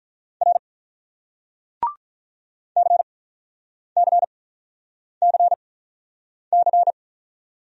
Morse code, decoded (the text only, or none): R CCCC